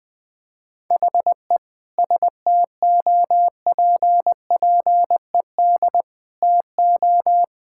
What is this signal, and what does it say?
Morse code, heard as a keyed tone, about 21 wpm, 700 Hz. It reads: HE STOPPED TO
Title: Sentences with 2-Word Phrases - 20wpm